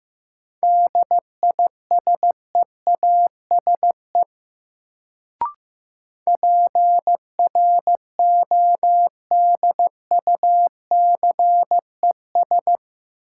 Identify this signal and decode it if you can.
Morse code, heard as a keyed tone, about 15 wpm, 700 Hz. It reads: DISEASE PRODUCES
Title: Sentences from Top 1000 Words - Review Part 6 - New Word Drill - 15wpm